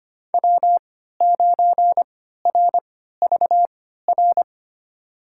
Morse code, heard as a keyed tone, about 25 wpm, spaced as for 14 wpm, 700 Hz. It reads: W9RVR
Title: USA Call Signs - 5 Characters - 3x Character Spacing - 25wpm